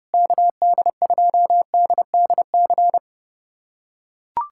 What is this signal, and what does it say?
Morse code, heard as a keyed tone, about 30 wpm, 700 Hz. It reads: KD2DDC
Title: USA Callsign DE Callsign 30wpm